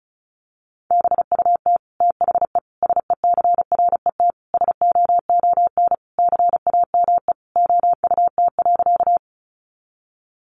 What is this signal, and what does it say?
Morse code, heard as a keyed tone, about 34 wpm, 700 Hz. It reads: BUT THE SECRET SOON CAME OUT.
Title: Sentences from Top 1200 Words 35wpm